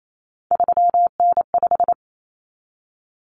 Morse code, heard as a keyed tone, about 28 wpm, 700 Hz. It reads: 3N5
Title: Common Cut Numbers - 3-Digit 28wpm